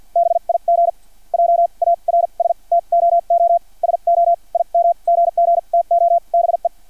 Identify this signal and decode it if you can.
Morse code, heard as a keyed tone, about 50 wpm, spaced as for 40 wpm, 680 Hz. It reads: BIG JARS TOO SO IM GGTO BE
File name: archived / CRSnet